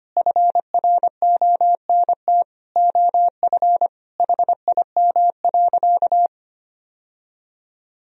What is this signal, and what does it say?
Morse code, heard as a keyed tone, about 25 wpm, 700 Hz. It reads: FRONT OF HIM.
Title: Sentences from Top 1000 Words - Review Part 1 - 25wpm